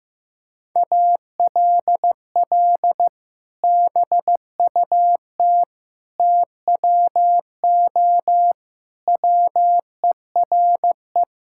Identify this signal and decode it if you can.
Morse code, heard as a keyed tone, about 15 wpm, 700 Hz. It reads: ALL BUT TWO WERE
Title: Sentences from Top 1000 Words - Review Part 5 - 15wpm